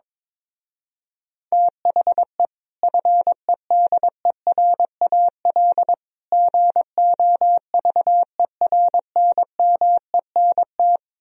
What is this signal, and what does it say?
Morse code, heard as a keyed tone, about 22 wpm, 700 Hz. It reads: THE FEDERAL GOVERNMENT
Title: Speed-Racing - Sets of 3 Words 22wpm